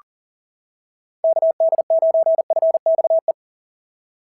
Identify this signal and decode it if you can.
Morse code, heard as a keyed tone, about 40 wpm, 690 Hz. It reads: KD9FXE
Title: SOTA QSO Practice 40wpm